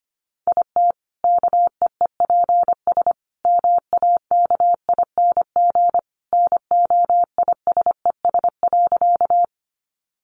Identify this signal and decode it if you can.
Morse code, heard as a keyed tone, about 25 wpm, 700 Hz. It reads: IT KEEPS MAKING NOISES.
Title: Additional Easy Phrases 25wpm